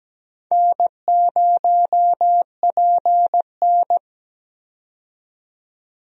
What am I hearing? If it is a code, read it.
Morse code, heard as a keyed tone, about 17 wpm, 700 Hz. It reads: N0PN
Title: USA Call Signs - 4 Characters - 17wpm